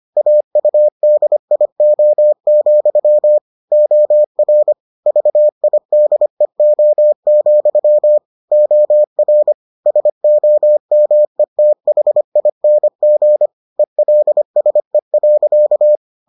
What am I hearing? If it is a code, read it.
Morse code, heard as a keyed tone, about 25 wpm, 600 Hz. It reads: AUDIO, OR VIDEO, OR SOMETHING ELSE.